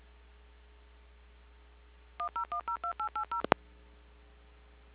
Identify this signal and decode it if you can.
DTMF, heard as telephone touch tones, tones of about 100 ms. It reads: 10102880